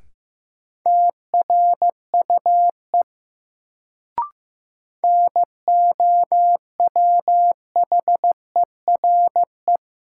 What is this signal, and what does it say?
Morse code, heard as a keyed tone, about 15 wpm, 700 Hz. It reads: TRUE NOWHERE
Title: Sets of 5 Words Encore - New Word Drill - 15wpm